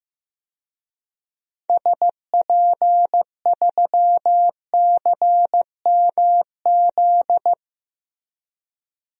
Morse code, heard as a keyed tone, about 16 wpm, 700 Hz. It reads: SP3CMZ
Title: DXCC-100 Call Signs - 6 Characters - Repeated 2x - 15wpm